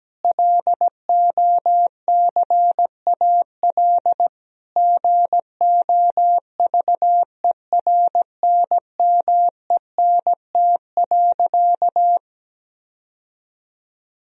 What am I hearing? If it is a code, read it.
Morse code, heard as a keyed tone, about 17 wpm, 700 Hz. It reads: LOCAL GOVERNMENT.
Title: Sentences from Top 1000 Words - Review Part 3 - 17wpm